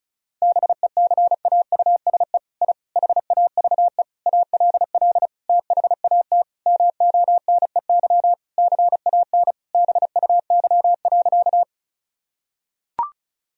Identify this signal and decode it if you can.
Morse code, heard as a keyed tone, about 34 wpm, 700 Hz. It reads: BECAUSE I HAVE ALL THAT MONEY CAN BUY.
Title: Sentences from Top 1000 Words - Review Part 4 - 35wpm